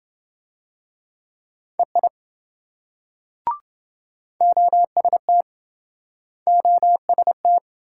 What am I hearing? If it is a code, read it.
Morse code, heard as a keyed tone, about 27 wpm, 700 Hz. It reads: EI OST OST